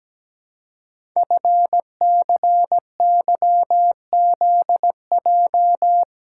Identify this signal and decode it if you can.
Morse code, heard as a keyed tone, about 17 wpm, 700 Hz. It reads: FCYZJ
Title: Code Groups - 5 Letters - 17wpm